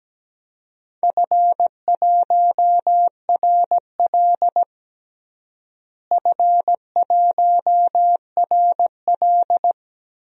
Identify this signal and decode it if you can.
Morse code, heard as a keyed tone, about 17 wpm, 700 Hz. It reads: F1RL F1RL
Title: DXCC-100 Call Signs - 4 Characters - Repeated 2x - 17wpm